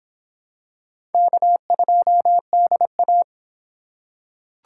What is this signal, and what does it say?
Morse code, heard as a keyed tone, about 26 wpm, 700 Hz. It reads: K2DA